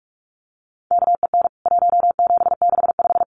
Morse code, heard as a keyed tone, about 44 wpm, 700 Hz. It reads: KEN 1765
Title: CWT Exchange 45wpm